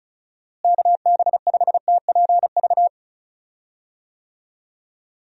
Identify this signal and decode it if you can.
Morse code, heard as a keyed tone, about 36 wpm, 700 Hz. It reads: KB5TPV
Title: USA Call Signs - 6 Characters - Repeated 3x - 35wpm